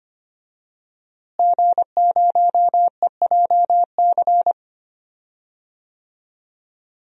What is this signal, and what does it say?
Morse code, heard as a keyed tone, about 25 wpm, 700 Hz. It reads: G0EJC
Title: DXCC-100 Call Signs - 5 Characters - Repeated 3x - 25wpm